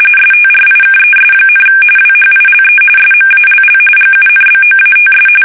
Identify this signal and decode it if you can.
RTTY, about 45.45 baud. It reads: STRIKES, WHILE SMALL GUNBOATS
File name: fsk850 / usnavy5